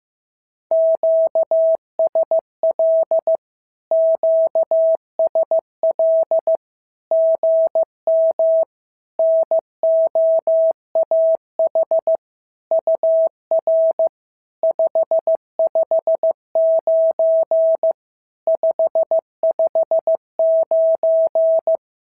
Morse code, heard as a keyed tone, about 15 wpm, 690 Hz. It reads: QSL QSL GM NOAH UR 559 559
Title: SOTA QSO Practice 15wpm